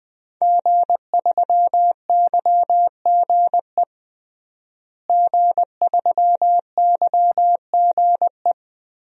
Morse code, heard as a keyed tone, about 21 wpm, 700 Hz. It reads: G3YGE G3YGE
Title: DXCC-100 Call Signs - 5 Characters - Repeated 2x - 20wpm